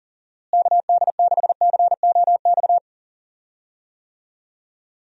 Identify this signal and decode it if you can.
Morse code, heard as a keyed tone, about 40 wpm, 700 Hz. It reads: KD6COX
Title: USA Call Signs - 6 Characters - Repeated 2x - 40wpm